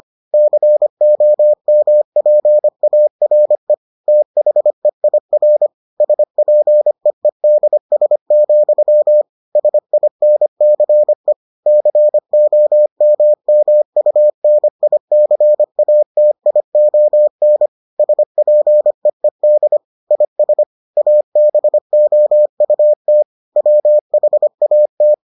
Morse code, heard as a keyed tone, about 25 wpm, 600 Hz. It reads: COMPARE THEIR SPEEDS, SINCE COMMUNICATION SPEED IS ABOUT WHAT